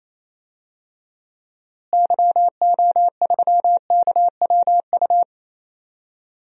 Morse code, heard as a keyed tone, about 28 wpm, 700 Hz. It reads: YO3KWU
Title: DXCC-100 Call Signs - 6 Characters - Repeated 2x - 28wpm